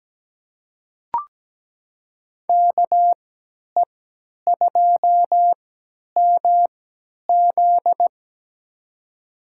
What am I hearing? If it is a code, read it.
Morse code, heard as a keyed tone, about 17 wpm, spaced as for 10 wpm, 700 Hz. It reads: KE2MZ
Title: USA Call Signs - 5 Characters - 3x Character Spacing - 17wpm